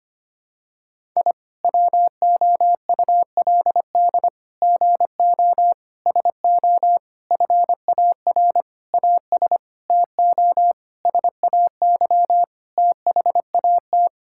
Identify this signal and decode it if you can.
Morse code, heard as a keyed tone, about 25 wpm, 700 Hz. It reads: I WOULD GO SO FAR AS TO SAY THAT
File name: Sets of 9 Words - 25wpm